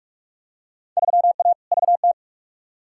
Morse code, heard as a keyed tone, about 44 wpm, 700 Hz. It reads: 3A VT
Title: Field Day Exchanges Encore - 45wpm